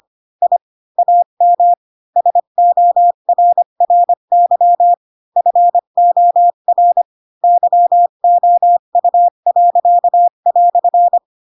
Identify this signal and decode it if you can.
Morse code, heard as a keyed tone, about 25 wpm, 700 Hz. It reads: I AM SORRY FOR YOU."
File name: Pinocchio / Pinocchio ch4 20 25 ik1hge